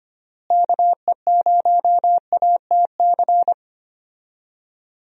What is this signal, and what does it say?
Morse code, heard as a keyed tone, about 25 wpm, 700 Hz. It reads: KE0ATC
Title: USA Call Signs - 6 Characters - Repeated 3x - 25wpm